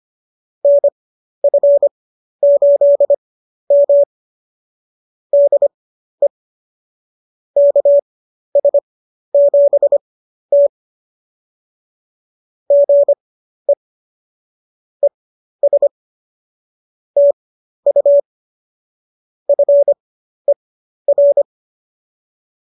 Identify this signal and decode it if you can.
Morse code, heard as a keyed tone, about 25 wpm, spaced as for 12 wpm, 590 Hz. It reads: NF8M DE KS7T GE ES TU FER